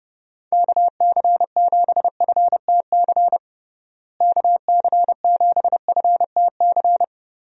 Morse code, heard as a keyed tone, about 31 wpm, 700 Hz. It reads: KC7FTC KC7FTC
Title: USA Call Signs - 6 Characters - Repeated 3x - 30wpm